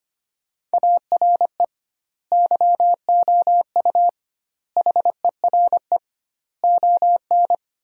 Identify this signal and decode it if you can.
Morse code, heard as a keyed tone, about 25 wpm, 700 Hz. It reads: ARE YOU HERE ON
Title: 2x Word-Spacing - Sentences from Top 400 Words - 25wpm